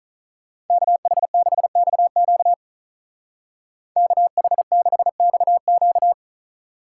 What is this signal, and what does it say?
Morse code, heard as a keyed tone, about 43 wpm, 700 Hz. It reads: KH6XQ KH6XQ
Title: Speed Racing - USA Call Signs 35wpm